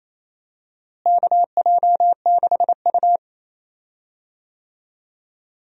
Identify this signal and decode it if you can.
Morse code, heard as a keyed tone, about 28 wpm, 700 Hz. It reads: KJ6U
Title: USA Call Signs - 4 Characters - Repeated 2x - 28wpm